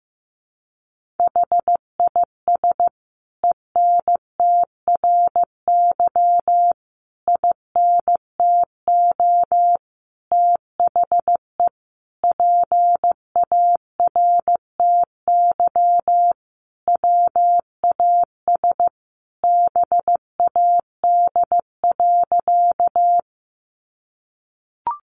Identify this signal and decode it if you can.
Morse code, heard as a keyed tone, about 15 wpm, 700 Hz. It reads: HIS ENTRY INTO THE PARTY WAS BAD.
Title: Sentences from Top 1500 Words 15wpm